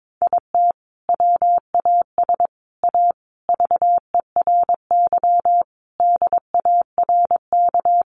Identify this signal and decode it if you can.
Morse code, heard as a keyed tone, about 22 wpm, 700 Hz. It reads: IT WAS A VERY DARK
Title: Sentences from Top 500 Words 22wpm